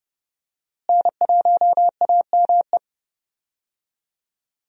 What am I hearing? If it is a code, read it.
Morse code, heard as a keyed tone, about 30 wpm, 700 Hz. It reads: N1AME
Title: USA Call Signs - 5 Characters - Repeated 2x - 30wpm